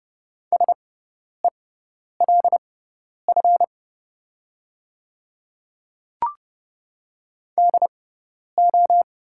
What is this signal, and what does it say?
Morse code, heard as a keyed tone, about 30 wpm, spaced as for 10 wpm, 690 Hz. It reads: SELF DO
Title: 6x Character Spacing - Top 200 Words - 30wpm